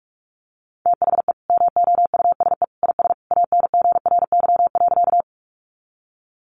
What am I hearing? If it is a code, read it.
Morse code, heard as a keyed tone, about 44 wpm, 700 Hz. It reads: THE MOUSE IS ANGRY.
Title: Easy Phrases 45wpm